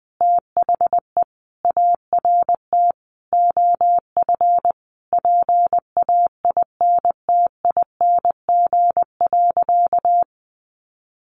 Morse code, heard as a keyed tone, about 20 wpm, 700 Hz. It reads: THE ART OF PAINTING.
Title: Sentences from Top 1200 Words 20wpm